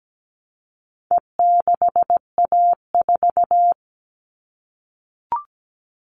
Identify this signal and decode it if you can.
Morse code, heard as a keyed tone, about 17 wpm, 700 Hz. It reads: E6A4
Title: All Cut Numbers - 4-Digit 17wpm